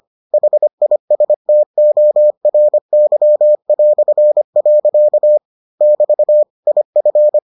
Morse code, heard as a keyed tone, about 25 wpm, 600 Hz. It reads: HISTORY". <BT> IF